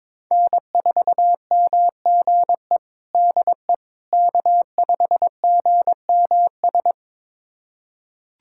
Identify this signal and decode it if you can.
Morse code, heard as a keyed tone, about 22 wpm, 700 Hz. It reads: N4MGE DE K5GMS
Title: USA Callsign DE Callsign - 5 Characters - 22wpm